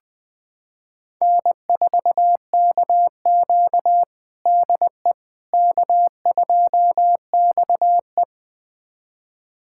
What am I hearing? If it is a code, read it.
Morse code, heard as a keyed tone, about 21 wpm, 700 Hz. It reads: N4KQ DE K2XE
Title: USA Callsign DE Callsign - 4 Characters - 20wpm